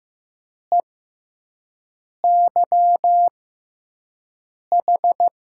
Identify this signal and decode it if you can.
Morse code, heard as a keyed tone, about 15 wpm, spaced as for 5 wpm, 700 Hz. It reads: EYH
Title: USA Call Signs - 5 Characters - 6x Character Spacing - 15wpm